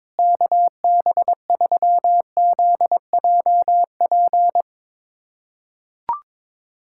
Morse code, heard as a keyed tone, about 22 wpm, 700 Hz. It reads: KB3ZJP
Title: USA Callsign DE Callsign 22wpm